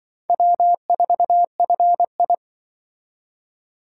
Morse code, heard as a keyed tone, about 24 wpm, 700 Hz. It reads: W4FI
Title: Speed Racing - USA 1x2 Call Signs 20wpm